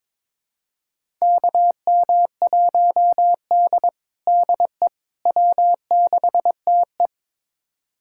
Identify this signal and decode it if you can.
Morse code, heard as a keyed tone, about 22 wpm, 700 Hz. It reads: KM1D DE W6TE
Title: USA Callsign DE Callsign - 4 Characters - 22wpm